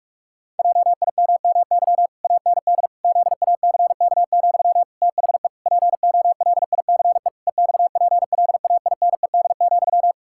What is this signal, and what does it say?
Morse code, heard as a keyed tone, about 46 wpm, 700 Hz. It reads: JIMMY AND ZACK, THE POLICE EXPLAINED,
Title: Pangram Challenge v2 - 45wpm